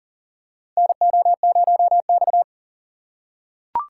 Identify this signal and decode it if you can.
Morse code, heard as a keyed tone, about 40 wpm, 700 Hz. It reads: NO0X
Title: USA Call Signs - 4 Characters - 4x Character Spacing - 40wpm